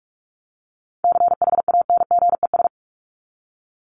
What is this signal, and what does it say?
Morse code, heard as a keyed tone, about 44 wpm, 700 Hz. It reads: CHANGES
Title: ES:S Suffix - Top 100 Words with Base Words 45wpm